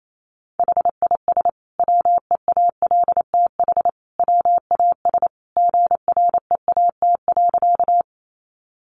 Morse code, heard as a keyed tone, about 28 wpm, 700 Hz. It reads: HIS WEALTH WAS GREAT.
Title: Sentences from Top 1200 Words 28wpm